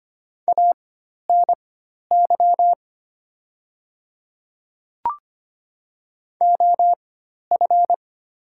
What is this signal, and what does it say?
Morse code, heard as a keyed tone, about 25 wpm, spaced as for 12 wpm, 690 Hz. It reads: ANY OF
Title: 4x Character Spacing - Top 200 Words - 25wpm